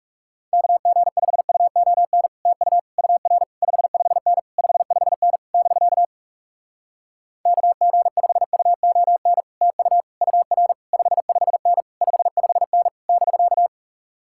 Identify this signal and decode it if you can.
Morse code, heard as a keyed tone, about 37 wpm, 700 Hz. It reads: KG5UON TU UR 55N 55N <BK> KG5UON TU UR 55N 55N <BK>
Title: Speed-Racing Common QSO Phrases Encore - 30wpm